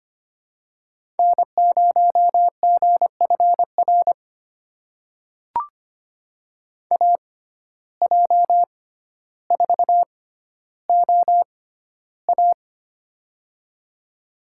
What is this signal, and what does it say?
Morse code, heard as a keyed tone, about 25 wpm, spaced as for 13 wpm, 700 Hz. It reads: N0GFR AJ4OA
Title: USA Call Signs - 5 Characters - 6x Character Spacing - 25wpm